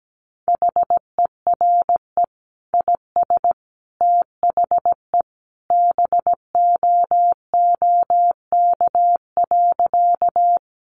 Morse code, heard as a keyed tone, about 17 wpm, 700 Hz. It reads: HERE IS THE BOOK.
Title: Sentences from Top 300 Words 17wpm